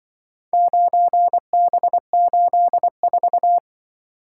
Morse code, heard as a keyed tone, about 23 wpm, 700 Hz. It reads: 9B84
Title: Speed-Racing - Four Letter-Number Combinations - 20wpm